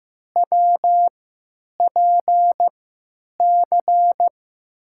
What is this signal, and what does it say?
Morse code, heard as a keyed tone, about 15 wpm, spaced as for 9 wpm, 700 Hz. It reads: WPC
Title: USA Call Signs - 6 Characters - 3x Character Spacing - 15wpm